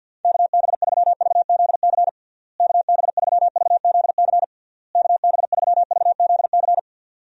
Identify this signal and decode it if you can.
Morse code, heard as a keyed tone, about 51 wpm, 700 Hz. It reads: KB3VZC KB3VZC KB3VZC
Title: USA Call Signs - 6 Characters - Repeated 3x - 50wpm